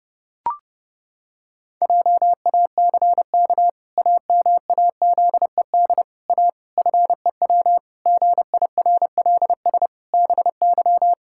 Morse code, heard as a keyed tone, about 30 wpm, 700 Hz. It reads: JACK AMAZED A FEW GIRLS BY
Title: Pangram Challenge v2 - 30wpm